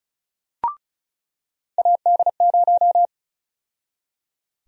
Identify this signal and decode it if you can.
Morse code, heard as a keyed tone, about 36 wpm, 700 Hz. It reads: AD0
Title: USA Call Sign Prefixes - 35wpm